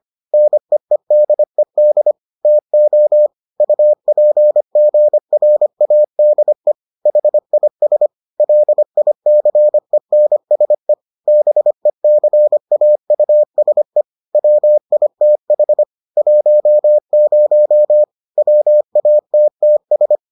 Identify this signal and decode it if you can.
Morse code, heard as a keyed tone, about 25 wpm, 600 Hz. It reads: NEEDED TO UPGRADE HIS LICENSE BECAUSE WITH 10 WATTS